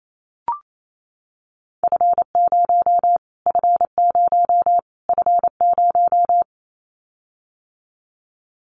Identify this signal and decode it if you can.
Morse code, heard as a keyed tone, about 28 wpm, 700 Hz. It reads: F0 F0 F0
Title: Callsign Prefixes for Top 25 DXCC - Repeated 3x - 28wpm